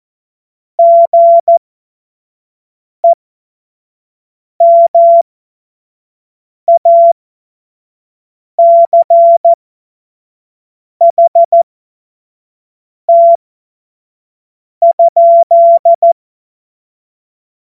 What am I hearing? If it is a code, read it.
Morse code, heard as a keyed tone, about 14 wpm, spaced as for 5 wpm, 680 Hz. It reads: GEMACHT?